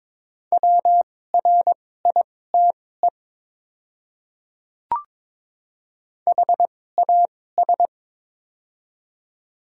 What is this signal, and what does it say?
Morse code, heard as a keyed tone, about 22 wpm, spaced as for 16 wpm, 700 Hz. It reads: WRITE HAS
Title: 2x Character Spacing - Top 300 Words - 22wpm